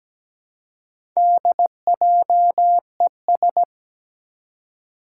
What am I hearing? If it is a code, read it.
Morse code, heard as a keyed tone, about 17 wpm, 700 Hz. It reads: DJES
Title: Four Challenging Letter-Number Combinations 17wpm